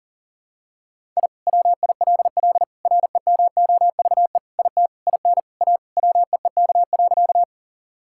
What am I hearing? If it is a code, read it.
Morse code, heard as a keyed tone, about 37 wpm, 700 Hz. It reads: I WILL REMOVE IT IN A WEEK.
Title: Sentences from Top 1000 Words - Review Part 5 - 40wpm